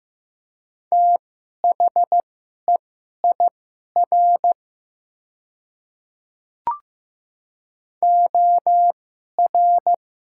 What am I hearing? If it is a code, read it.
Morse code, heard as a keyed tone, about 15 wpm, spaced as for 11 wpm, 700 Hz. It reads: THEIR OR
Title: 2x Character Spacing - Top 100 Words - 15wpm